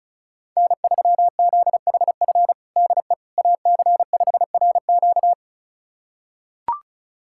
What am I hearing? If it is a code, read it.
Morse code, heard as a keyed tone, about 36 wpm, 700 Hz. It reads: N3ZHF DE AC5RQ
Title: USA Callsign DE Callsign - 5 Characters - 35wpm